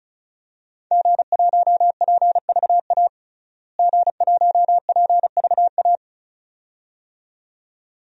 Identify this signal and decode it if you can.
Morse code, heard as a keyed tone, about 36 wpm, 700 Hz. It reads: G1PVA G1PVA
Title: DXCC-100 Call Signs - 5 Characters - Repeated 2x - 35wpm